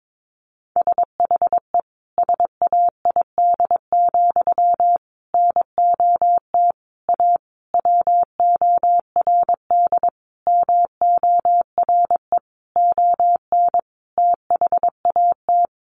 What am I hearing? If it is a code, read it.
Morse code, heard as a keyed tone, about 22 wpm, 700 Hz. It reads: SHE SAID, NOT A WORD MORE ON THAT
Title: Sentences from Top 800 Words 22wpm